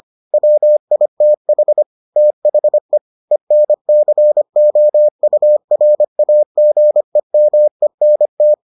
Morse code, heard as a keyed tone, about 25 wpm, 600 Hz. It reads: WITH THE ENCOURAGEMENT